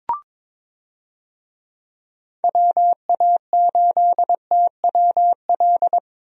Morse code, heard as a keyed tone, about 22 wpm, 700 Hz. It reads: WA8TWL
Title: USA Call Signs - 6 Characters - Repeated 3x - 22wpm